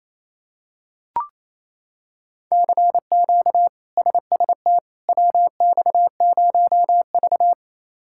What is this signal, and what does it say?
Morse code, heard as a keyed tone, about 28 wpm, 700 Hz. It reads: CQ SST WX0V
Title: K1USN SST QSO Practice - 28wpm